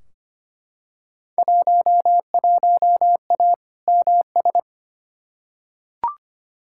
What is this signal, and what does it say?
Morse code, heard as a keyed tone, about 25 wpm, 700 Hz. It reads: 11A MS
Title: Field Day Exchanges Encore - 25wpm